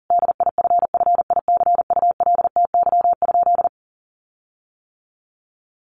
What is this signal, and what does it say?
Morse code, heard as a keyed tone, about 40 wpm, 700 Hz. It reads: DIFFICULTY?
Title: Sentences from Top 1500 Words 40wpm